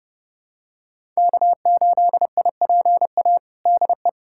Morse code, heard as a keyed tone, about 30 wpm, 700 Hz. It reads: K8IPA DE
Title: USA Callsign DE Callsign - 5 Characters - 30wpm